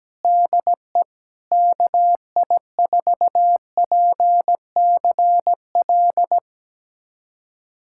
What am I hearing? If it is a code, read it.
Morse code, heard as a keyed tone, about 17 wpm, 700 Hz. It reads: DE KI4PCL
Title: USA Callsign DE Callsign 17wpm